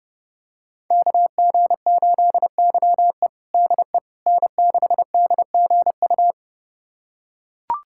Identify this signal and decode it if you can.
Morse code, heard as a keyed tone, about 30 wpm, 700 Hz. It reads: KG8YE DE N6DGU
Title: USA Callsign DE Callsign - 5 Characters - 30wpm